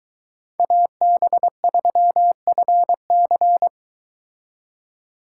Morse code, heard as a keyed tone, about 23 wpm, 700 Hz. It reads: AB3FC